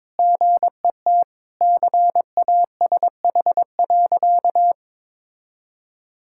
Morse code, heard as a keyed tone, about 22 wpm, 700 Hz. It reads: GET CASH.
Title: Sentences from Top 1000 Words - Review Part 5 - 22wpm